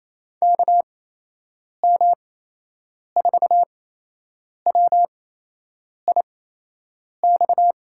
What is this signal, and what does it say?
Morse code, heard as a keyed tone, about 27 wpm, spaced as for 8 wpm, 700 Hz. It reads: KM4WIX
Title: USA Call Signs - 6 Characters - 8x Character Spacing - 28wpm